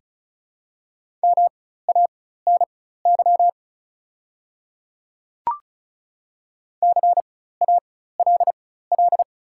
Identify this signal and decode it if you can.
Morse code, heard as a keyed tone, about 36 wpm, spaced as for 16 wpm, 700 Hz. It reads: MANY CALL
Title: 4x Character Spacing - Top 100 Words - 35wpm